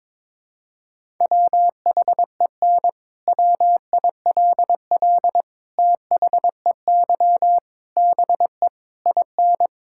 Morse code, heard as a keyed tone, about 22 wpm, 700 Hz. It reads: WHEN WILL THEY BE IN
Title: Sentences from Top 1000 Words - Review Part 6 - 22wpm